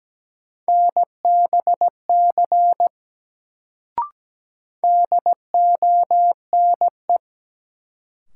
Morse code, heard as a keyed tone, about 17 wpm, 700 Hz. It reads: NBC DONE
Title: Sets of 5 Words Encore - New Word Drill - 17wpm